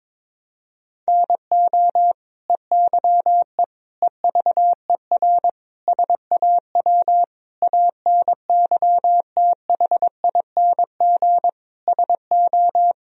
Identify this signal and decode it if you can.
Morse code, heard as a keyed tone, about 22 wpm, 700 Hz. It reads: NO EYE EVER SAW ANYTHING SO
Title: Sentences from Top 1000 Words - Review Part 1 - 22wpm